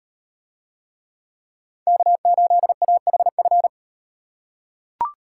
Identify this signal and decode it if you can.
Morse code, heard as a keyed tone, about 36 wpm, 700 Hz. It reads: K8AHF